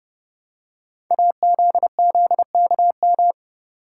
Morse code, heard as a keyed tone, about 30 wpm, 700 Hz. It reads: AZZKM
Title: Code Groups - 5 Letters - 30wpm